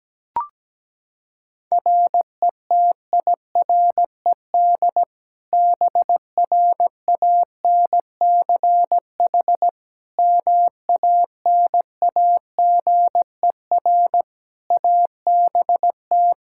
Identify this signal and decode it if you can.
Morse code, heard as a keyed tone, about 17 wpm, 700 Hz. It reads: RETIRED BRANCH MANAGER ABT
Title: Practice QSO - Exchange 3 - More Info - 17wpm